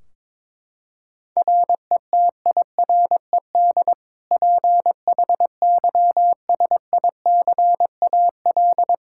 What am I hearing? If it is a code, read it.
Morse code, heard as a keyed tone, about 21 wpm, 700 Hz. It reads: RETIRED PHYSICAL
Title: Practice QSO - Exchange 3 - More Info - 22wpm